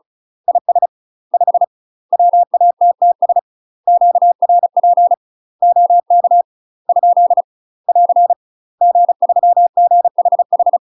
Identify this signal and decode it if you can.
Morse code, heard as a keyed tone, about 34 wpm, 700 Hz. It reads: IS 5 WATTS QRP OK ? <AR> G3GHH
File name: UK MASTER QSOs TRAINING / 35WPM / UK MASTER QSOs 0007